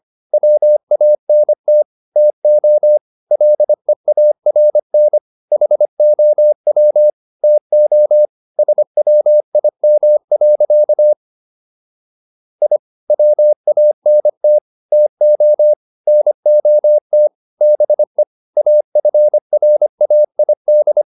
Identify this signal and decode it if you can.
Morse code, heard as a keyed tone, about 25 wpm, 600 Hz. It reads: WANT TO LEARN HOW TO SWIM. I WANT TO NOT BE AFRAID